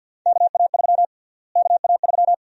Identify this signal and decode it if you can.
Morse code, heard as a keyed tone, about 52 wpm, 700 Hz. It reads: KA3 KA3
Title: USA Call Sign Prefixes - Repeated 2x - 50wpm